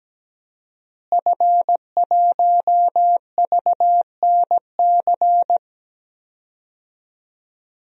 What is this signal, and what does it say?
Morse code, heard as a keyed tone, about 17 wpm, 700 Hz. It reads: F1VNC
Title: DXCC-100 Call Signs - 5 Characters - Repeated 3x - 17wpm